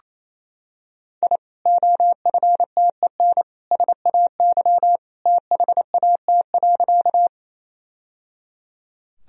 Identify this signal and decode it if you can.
Morse code, heard as a keyed tone, about 28 wpm, 700 Hz. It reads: I OFTEN SAY THAT.
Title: Sentences from Top 1000 Words - Review Part 1 - 28wpm